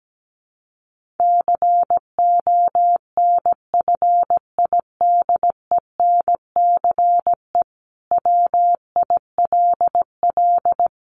Morse code, heard as a keyed tone, about 17 wpm, 700 Hz. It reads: CONFIDENCE WILL
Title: Sentences from Top 1500 Words 17wpm